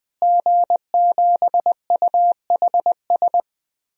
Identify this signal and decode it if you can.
Morse code, heard as a keyed tone, about 20 wpm, 700 Hz. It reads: G7UHS
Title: DXCC-100 Call Signs - 5 Characters - Repeated 3x - 20wpm